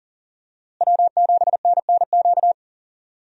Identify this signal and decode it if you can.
Morse code, heard as a keyed tone, about 40 wpm, 700 Hz. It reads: W7NNQ
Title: USA Call Signs - 5 Characters - Repeated 2x - 40wpm